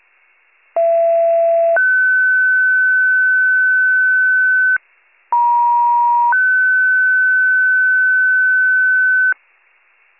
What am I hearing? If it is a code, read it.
Two-tone page, heard as a one-second tone, then a three-second tone, about 670 then 1500 Hz, about 950 then 1500 Hz. It is heard from 0.8 to 9.3 s.